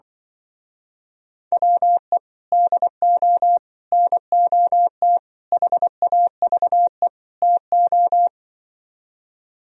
Morse code, heard as a keyed tone, about 24 wpm, 700 Hz. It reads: WE DO NOT HAVE TO
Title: Speed-Racing - Sets of 5 Words 20wpm